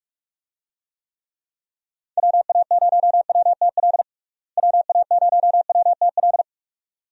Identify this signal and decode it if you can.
Morse code, heard as a keyed tone, about 46 wpm, 700 Hz. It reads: WA0WTL WA0WTL
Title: USA Call Signs - 6 Characters - Repeated 2x - 45wpm